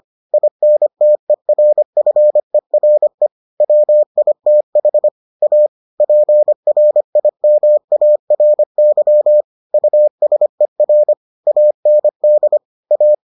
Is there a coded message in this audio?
Yes: Morse code from 0.3 to 13.1 s, heard as a keyed tone, about 25 wpm, 600 Hz. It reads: INTERFERE WITH A PRIMARY USER AND A